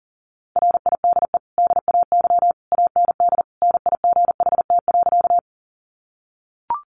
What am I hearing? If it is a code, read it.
Morse code, heard as a keyed tone, about 40 wpm, 700 Hz. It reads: RIDE DAY AND NIGHT.
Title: Sentences from Top 1200 Words 40wpm